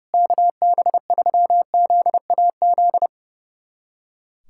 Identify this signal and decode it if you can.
Morse code, heard as a keyed tone, about 32 wpm, 700 Hz. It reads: KB3ZAZ
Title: USA Callsign DE Callsign - 6 Characters - 30wpm